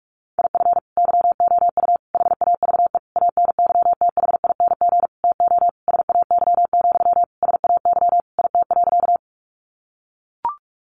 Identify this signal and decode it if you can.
Morse code, heard as a keyed tone, about 44 wpm, 700 Hz. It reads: IF YOU HAVE ANYTHING TO SAY, SAY IT.
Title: Sentences from Top 1000 Words 45wpm